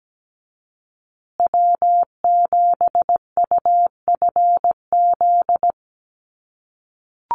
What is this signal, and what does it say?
Morse code, heard as a keyed tone, about 17 wpm, 700 Hz. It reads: W7UFZ
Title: USA Call Signs Repeated 3x 17wpm